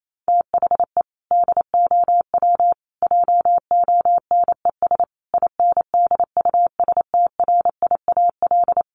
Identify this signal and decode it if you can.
Morse code, heard as a keyed tone, about 28 wpm, 700 Hz. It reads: THE DOW JONES INDUSTRIAL
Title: Sets of 5 Words 28wpm